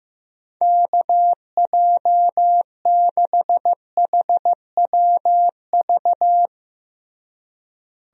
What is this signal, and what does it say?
Morse code, heard as a keyed tone, about 15 wpm, 700 Hz. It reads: KJ6HWV
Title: USA Call Signs - 6 Characters - Repeated 2x - 15wpm